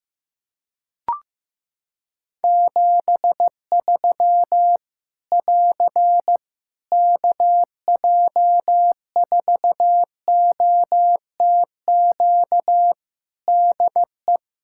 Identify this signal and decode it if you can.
Morse code, heard as a keyed tone, about 15 wpm, 700 Hz. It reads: 73 <AR> KJ4OTQ DE
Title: Practice QSO - Exchange 4 - Ending - 15wpm